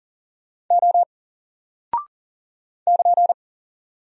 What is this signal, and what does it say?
Morse code, heard as a keyed tone, about 40 wpm, 700 Hz. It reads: O <KN>
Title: Single Character (Letter:Number:Punctuation:Prosign) Alt 40wpm